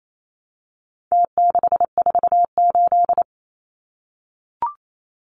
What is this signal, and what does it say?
Morse code, heard as a keyed tone, about 28 wpm, 700 Hz. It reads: T648
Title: Numbers - Contest Style - 4-Digit 28wpm